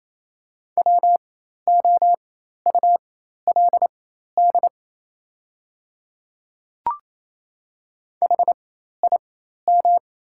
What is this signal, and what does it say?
Morse code, heard as a keyed tone, about 28 wpm, spaced as for 13 wpm, 690 Hz. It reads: WOULD HIM
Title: 4x Character Spacing - Top 100 Words - 28wpm